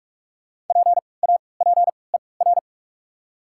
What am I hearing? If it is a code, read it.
Morse code, heard as a keyed tone, about 46 wpm, spaced as for 26 wpm, 700 Hz. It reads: PAPER
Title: 3x Character Spacing - Top 500 Words - 45wpm